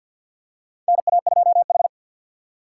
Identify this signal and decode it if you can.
Morse code, heard as a keyed tone, about 52 wpm, 700 Hz. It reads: NA2H
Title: USA Call Signs - 4 Characters - 3x Character Spacing - 50wpm